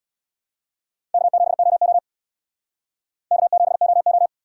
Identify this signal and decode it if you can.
Morse code, heard as a keyed tone, about 64 wpm, 700 Hz. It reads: K7OO K7OO